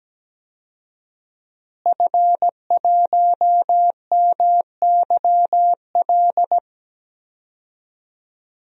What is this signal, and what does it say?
Morse code, heard as a keyed tone, about 17 wpm, 700 Hz. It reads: F1MYL
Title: DXCC-100 Call Signs - 5 Characters - Repeated 3x - 17wpm